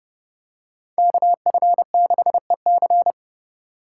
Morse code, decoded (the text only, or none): KF6EC